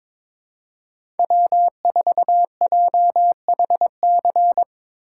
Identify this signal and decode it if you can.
Morse code, heard as a keyed tone, about 21 wpm, 700 Hz. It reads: W4JHC
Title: USA Call Signs - 5 Characters - 2x Character Spacing - 22wpm